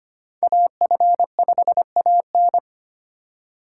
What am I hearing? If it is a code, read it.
Morse code, heard as a keyed tone, about 25 wpm, 700 Hz. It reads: AF5AN
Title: USA Call Signs - 4x Character Spacing - 25wpm